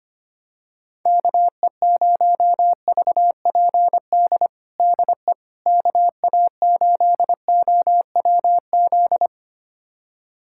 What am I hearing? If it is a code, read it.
Morse code, heard as a keyed tone, about 25 wpm, 700 Hz. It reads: KE0VPD DE KA8OWZ
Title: USA Callsign DE Callsign - 6 Characters - 25wpm